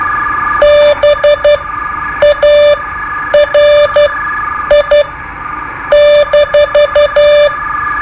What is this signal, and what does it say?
Morse code, heard as a keyed tone, about 12 wpm, spaced as for 8 wpm, 590 Hz. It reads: BARI-
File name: qso-01